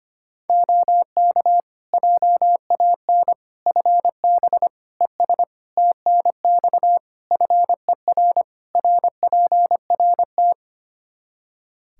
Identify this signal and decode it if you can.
Morse code, heard as a keyed tone, about 25 wpm, 700 Hz. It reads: OK JAN FB ES TNX FER RPRT
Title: K1USN SST QSO Practice 25wpm